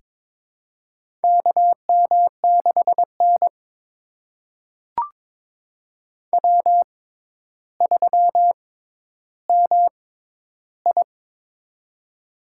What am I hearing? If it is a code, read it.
Morse code, heard as a keyed tone, about 22 wpm, spaced as for 11 wpm, 700 Hz. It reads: KM6N W3MI
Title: USA Call Signs - 4 Characters - 6x Character Spacing - 22wpm